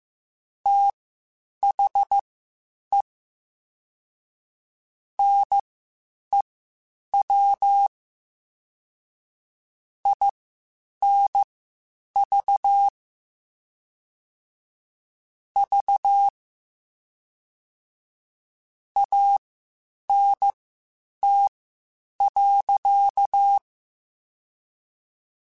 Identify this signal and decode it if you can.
Morse code, heard as a keyed tone, about 15 wpm, spaced as for 8 wpm, 800 Hz. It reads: THE NEW INV V ANT.
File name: (spaced) / QSOs Spaced-mp3s / WD0FBT 1 @13 sp